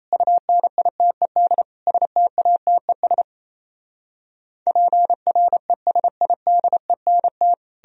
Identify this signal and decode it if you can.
Morse code, heard as a keyed tone, about 37 wpm, 700 Hz. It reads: UNITED STATES PRESIDENT